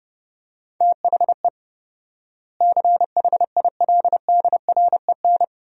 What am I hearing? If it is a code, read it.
Morse code, heard as a keyed tone, about 30 wpm, 700 Hz. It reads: THE CHILDREN
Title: 4x Word-Spacing - Sentences from Top 300 Words - 30wpm